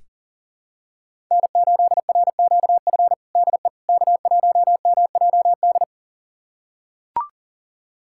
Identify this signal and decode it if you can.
Morse code, heard as a keyed tone, about 40 wpm, 700 Hz. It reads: N8RQF DE K1MJD
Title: USA Callsign DE Callsign - 5 Characters - 40wpm